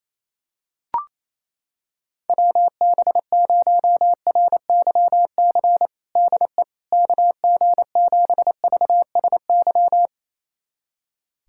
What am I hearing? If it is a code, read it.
Morse code, heard as a keyed tone, about 28 wpm, 700 Hz. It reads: WB0RYC DE KG7VSY
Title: USA Callsign DE Callsign 28wpm